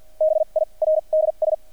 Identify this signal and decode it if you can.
Morse code, heard as a keyed tone, about 50 wpm, spaced as for 37 wpm, 630 Hz. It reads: BIRDS
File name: archived / CRSnet